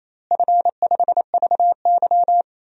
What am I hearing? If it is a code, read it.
Morse code, heard as a keyed tone, about 27 wpm, 700 Hz. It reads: F5VY
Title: DXCC-100 Call Signs - 4 Characters - Repeated 3x - 28wpm